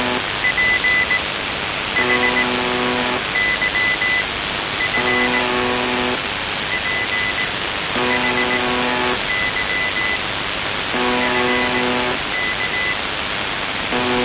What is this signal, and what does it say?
Morse code, heard as a keyed tone, about 18 wpm, spaced as for 9 wpm, 2000 Hz, with noise. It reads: PHYF PHY FK
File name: s28-4625usb- - -cw-bykompot